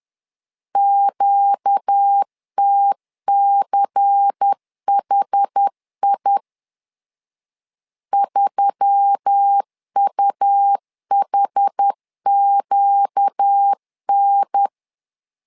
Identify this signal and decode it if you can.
Morse code, heard as a keyed tone, about 10 wpm, 800 Hz. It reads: QTCHI 3UHQN
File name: LEZIONI / 5 LBVC / 5 LBVC 10wpm 2349